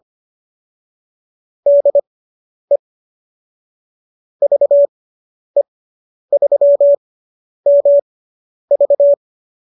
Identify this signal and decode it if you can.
Morse code, heard as a keyed tone, about 25 wpm, spaced as for 10 wpm, 570 Hz. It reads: DE VE3MV